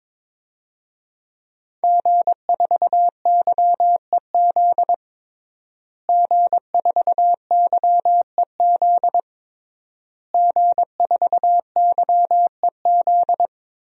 Morse code, heard as a keyed tone, about 22 wpm, 700 Hz. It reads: G4YEZ G4YEZ G4YEZ
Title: DXCC-100 Call Signs - 5 Characters - Repeated 3x - 22wpm